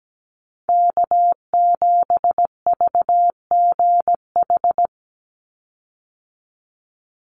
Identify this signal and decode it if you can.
Morse code, heard as a keyed tone, about 17 wpm, 700 Hz. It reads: K7VGH
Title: USA Call Signs 17wpm